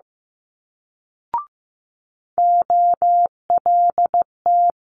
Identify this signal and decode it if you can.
Morse code, heard as a keyed tone, about 15 wpm, 700 Hz. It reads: OLT
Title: Multi-letter Phonemes Quick Fire 15wpm